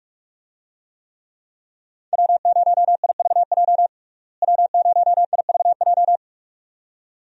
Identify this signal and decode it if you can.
Morse code, heard as a keyed tone, about 44 wpm, 700 Hz. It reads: W0IVJ W0IVJ
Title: USA Call Signs - 5 Characters - Repeated 2x - 45wpm